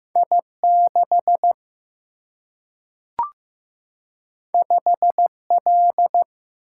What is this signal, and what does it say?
Morse code, heard as a keyed tone, about 15 wpm, 700 Hz. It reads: I6 5L
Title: Single-Letter Single-Number 15wpm